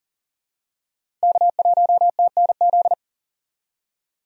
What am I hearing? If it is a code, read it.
Morse code, heard as a keyed tone, about 39 wpm, 700 Hz. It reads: K1TNZ
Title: USA Call Signs - 5 Characters - 40wpm